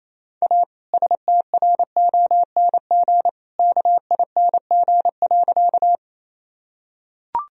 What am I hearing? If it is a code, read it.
Morse code, heard as a keyed tone, about 27 wpm, 700 Hz. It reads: A STRONG KING.
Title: Sentences from Top 1000 Words - Review Part 3 - 28wpm